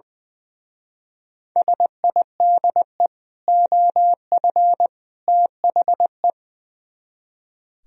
Morse code, heard as a keyed tone, about 23 wpm, 700 Hz. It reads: SIDE OF THE
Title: Speed-Racing - Sets of 3 Words 20wpm